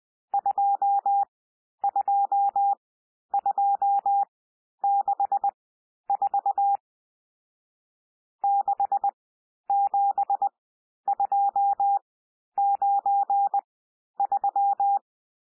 Morse code, heard as a keyed tone, about 20 wpm, spaced as for 11 wpm, 800 Hz. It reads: 22264 67293
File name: cwmp3 / N1 SOUND NUMBERS 20 wpm